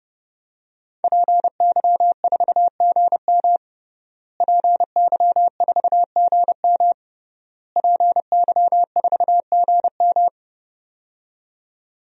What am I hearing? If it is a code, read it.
Morse code, heard as a keyed tone, about 32 wpm, 700 Hz. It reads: PY4GM PY4GM PY4GM
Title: DXCC-100 Call Signs - 5 Characters - Repeated 3x - 30wpm